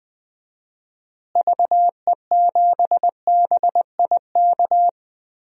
Morse code, heard as a keyed tone, about 19 wpm, 700 Hz. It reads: VE7BIK
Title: DXCC-100 Call Signs - 6 Characters - Repeated 3x - 20wpm